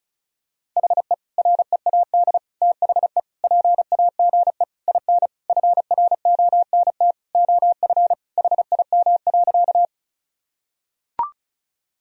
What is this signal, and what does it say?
Morse code, heard as a keyed tone, about 36 wpm, 700 Hz. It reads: HE READ THE PAGE IN FRONT OF HIM.
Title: Sentences from Top 1000 Words - Review Part 1 - 35wpm